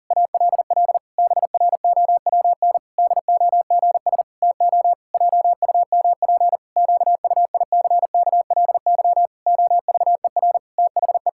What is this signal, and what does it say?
Morse code, heard as a keyed tone, about 40 wpm, 700 Hz. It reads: ALL BROWN DOGS TO JUMP QUICKLY OVER THE
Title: Pangram Challenge v2 - 40wpm